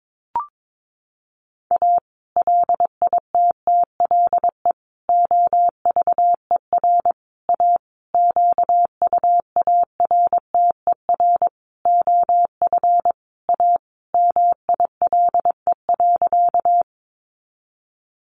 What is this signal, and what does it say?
Morse code, heard as a keyed tone, about 22 wpm, 700 Hz. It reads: A LITTLE OVER A QUARTER OF A MILE.
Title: Sentences from Top 1300 Words 22wpm